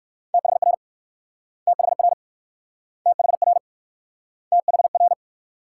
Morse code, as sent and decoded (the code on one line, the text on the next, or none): - .... .-. / - .... .-. / - .... .-. / - .... .-.
THR THR THR THR